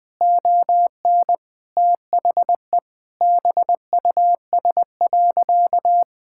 Morse code, heard as a keyed tone, about 19 wpm, 700 Hz. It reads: ON THE BUS.
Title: Sentences from Top 1000 Words - Review Part 6 - 20wpm